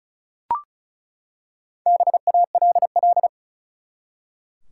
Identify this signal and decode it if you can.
Morse code, heard as a keyed tone, about 34 wpm, 700 Hz. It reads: BALL
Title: Four Letter Words - 35wpm